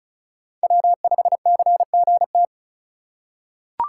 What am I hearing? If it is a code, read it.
Morse code, heard as a keyed tone, about 34 wpm, 700 Hz. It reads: W5CGT
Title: USA Call Signs - 5 Characters - 2x Character Spacing - 35wpm